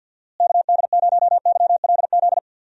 Morse code, heard as a keyed tone, about 50 wpm, 700 Hz. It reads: KD0YLZ
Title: USA Call Signs - 6 Characters - Repeated 2x - 50wpm